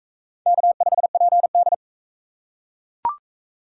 Morse code, heard as a keyed tone, about 44 wpm, 700 Hz. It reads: K5PD